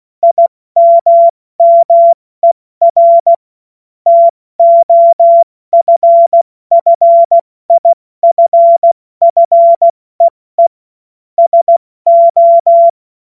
Morse code, heard as a keyed tone, about 16 wpm, 680 Hz. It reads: IMMER TOFFIFFEE SO
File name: Morsekurs / Witze / Witz 05. Oktober 2021, 70 F80